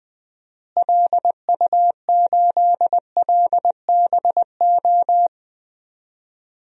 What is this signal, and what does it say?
Morse code, heard as a keyed tone, about 20 wpm, 700 Hz. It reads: LU8LBO
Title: DXCC-100 Call Signs - 6 Characters - Repeated 2x - 20wpm